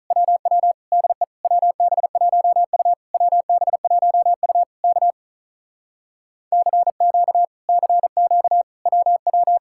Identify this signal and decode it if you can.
Morse code, heard as a keyed tone, about 47 wpm, 700 Hz. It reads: WW DE WB1U WB1U K CQ CQ WW